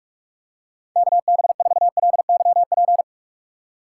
Morse code, heard as a keyed tone, about 47 wpm, 700 Hz. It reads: KB4LYP